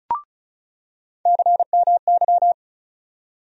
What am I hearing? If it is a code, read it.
Morse code, heard as a keyed tone, about 34 wpm, 700 Hz. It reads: CMY
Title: USA Call Sign Postfixes - 3 Letters 35wpm